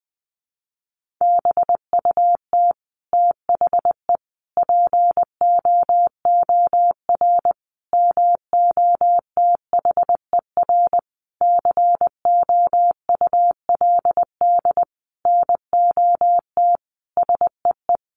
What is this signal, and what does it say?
Morse code, heard as a keyed tone, about 20 wpm, 700 Hz. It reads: BUT THE POOR MOTHER COULD NOT SEE
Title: Sentences from Top 1000 Words 20wpm